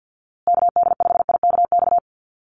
Morse code, heard as a keyed tone, about 50 wpm, 700 Hz. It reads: KD5IKX
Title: USA Call Signs Repeated 2x 50wpm